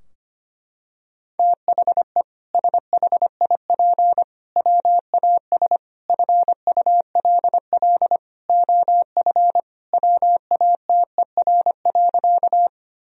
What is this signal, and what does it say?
Morse code, heard as a keyed tone, about 25 wpm, 700 Hz. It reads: THE SHIP WAS FULL OF WATER.
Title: Sentences from Top 1000 Words - Review Part 3 - 25wpm